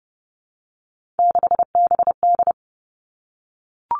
Morse code, heard as a keyed tone, about 30 wpm, 700 Hz. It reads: 6BD
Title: Three Challenging Letter-Number Combinations 30wpm